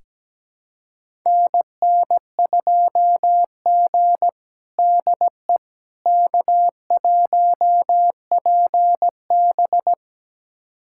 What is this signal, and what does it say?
Morse code, heard as a keyed tone, about 17 wpm, 700 Hz. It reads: NN2G DE K1PB
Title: USA Callsign DE Callsign - 4 Characters - 17wpm